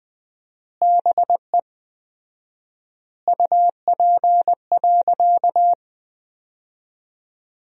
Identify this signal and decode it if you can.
Morse code, heard as a keyed tone, about 20 wpm, 700 Hz. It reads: BE UP.
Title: 4x Word-Spacing - Sentences from Top 300 Words - 20wpm